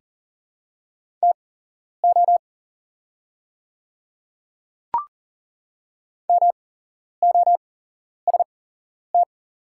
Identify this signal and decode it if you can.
Morse code, heard as a keyed tone, about 40 wpm, spaced as for 11 wpm, 690 Hz. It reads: TO MOST
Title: 8x Character Spacing - Top 100 Words - 40wpm